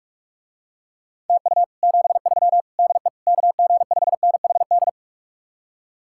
Morse code, heard as a keyed tone, about 48 wpm, 700 Hz. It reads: TU 73 DE KG5NHD